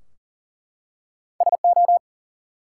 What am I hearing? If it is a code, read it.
Morse code, heard as a keyed tone, about 44 wpm, 700 Hz. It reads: SO
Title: Two Letter Words - 40wpm